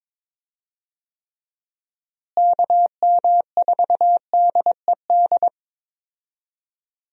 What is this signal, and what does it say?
Morse code, heard as a keyed tone, about 22 wpm, 700 Hz. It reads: KM4DED
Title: USA Call Signs - 6 Characters - Repeated 3x - 22wpm